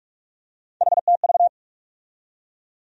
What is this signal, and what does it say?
Morse code, heard as a keyed tone, about 46 wpm, 700 Hz. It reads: HTV
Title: USA Call Sign Postfixes - 3 Letters 45wpm